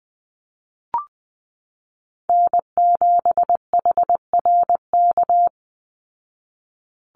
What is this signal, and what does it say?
Morse code, heard as a keyed tone, about 20 wpm, 700 Hz. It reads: N7HRK
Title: Call Signs Repeated 2x - 20wpm